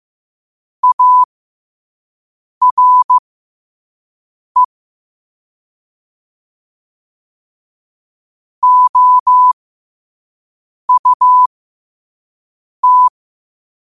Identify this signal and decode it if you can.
Morse code, heard as a keyed tone, about 15 wpm, spaced as for 5 wpm, 1000 Hz. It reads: ARE OUT
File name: qotdfiles / 5wpm